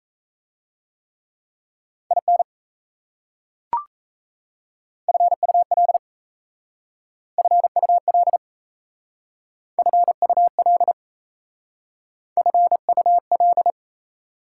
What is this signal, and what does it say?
Morse code, heard as a keyed tone, about 35 wpm, 690 Hz. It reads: IN FUL FUL FUL FUL
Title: Speed-Racing Prefixes and Suffixes 28wpm